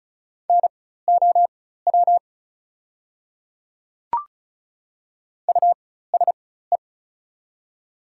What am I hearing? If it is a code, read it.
Morse code, heard as a keyed tone, about 34 wpm, spaced as for 16 wpm, 700 Hz. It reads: NOW USE
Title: 4x Character Spacing - Top 100 Words - 35wpm